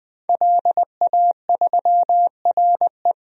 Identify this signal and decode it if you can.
Morse code, heard as a keyed tone, about 21 wpm, 700 Hz. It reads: LA3RE
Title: DXCC-100 Call Signs - 5 Characters - Repeated 2x - 20wpm